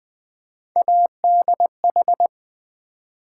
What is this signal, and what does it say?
Morse code, heard as a keyed tone, about 20 wpm, 700 Hz. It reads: ADH
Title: Build A Word - 3 Letters At A Time - 6-Letter Words - 20wpm